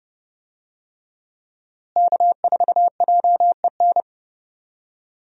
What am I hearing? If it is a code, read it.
Morse code, heard as a keyed tone, about 30 wpm, 700 Hz. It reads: K4JEN
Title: USA Call Signs - 5 Characters - Repeated 3x - 30wpm